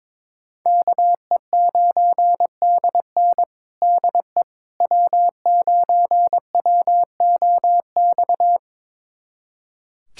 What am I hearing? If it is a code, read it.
Morse code, heard as a keyed tone, about 21 wpm, 700 Hz. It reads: KE9DN DE W9WOX
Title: USA Callsign DE Callsign - 5 Characters - 22wpm